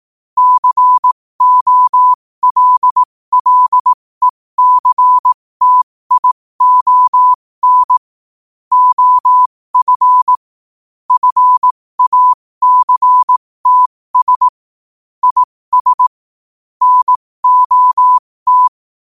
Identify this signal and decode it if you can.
Morse code, heard as a keyed tone, about 18 wpm, spaced as for 15 wpm, 1000 Hz. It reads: COLLECTION OF FACTS IS NOT
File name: qotdfiles / 15wpm